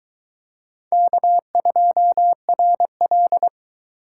Morse code, heard as a keyed tone, about 22 wpm, 700 Hz. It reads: K2RL